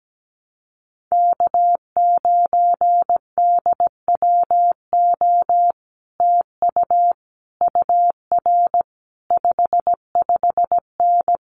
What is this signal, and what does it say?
Morse code, heard as a keyed tone, about 17 wpm, 700 Hz. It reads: K9DWO TU UR 55N
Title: Common QSO Phrases Encore 17wpm